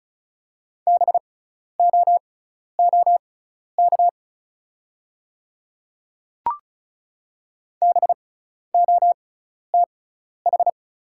Morse code, heard as a keyed tone, about 36 wpm, spaced as for 12 wpm, 690 Hz. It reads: BOOK BOTH
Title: 6x Character Spacing - Top 500 Words - 35wpm